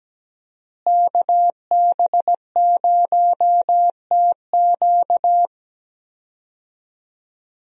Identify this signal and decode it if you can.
Morse code, heard as a keyed tone, about 17 wpm, 700 Hz. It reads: KB0TQ
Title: USA Call Signs - 5 Characters - Repeated 3x - 17wpm